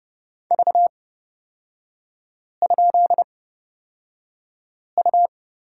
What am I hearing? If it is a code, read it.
Morse code, heard as a keyed tone, about 30 wpm, spaced as for 5 wpm, 700 Hz. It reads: V?U
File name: Lesson 36e - BT - Three Characters 30wpm